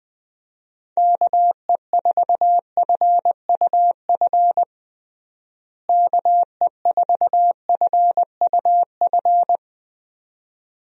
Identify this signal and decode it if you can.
Morse code, heard as a keyed tone, about 20 wpm, 700 Hz. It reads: KE4FUF KE4FUF
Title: USA Call Signs - 6 Characters - Repeated 2x - 20wpm